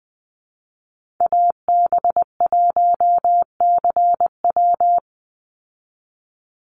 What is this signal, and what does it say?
Morse code, heard as a keyed tone, about 20 wpm, 700 Hz. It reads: AB1CW
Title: Call Signs Repeated 3x - 20wpm